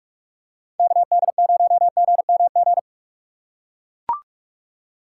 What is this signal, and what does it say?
Morse code, heard as a keyed tone, about 43 wpm, 700 Hz. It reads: KD0GMG
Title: USA Call Signs - 6x Character Spacing - 45wpm